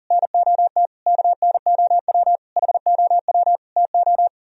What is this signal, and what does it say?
Morse code, heard as a keyed tone, about 52 wpm, 700 Hz. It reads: NOT KNOW HOW TO